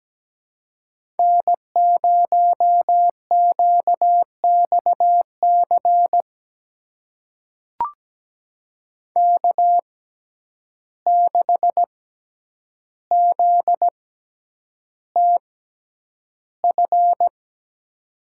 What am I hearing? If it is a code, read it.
Morse code, heard as a keyed tone, about 17 wpm, spaced as for 9 wpm, 700 Hz. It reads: N0QXC K6ZTF
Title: USA Call Signs - 5 Characters - 6x Character Spacing - 17wpm